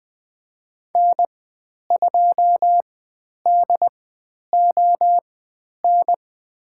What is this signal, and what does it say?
Morse code, heard as a keyed tone, about 20 wpm, spaced as for 10 wpm, 700 Hz. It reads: N2DON
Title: 095 Lesson 16g - Letter c - Call Signs 20wpm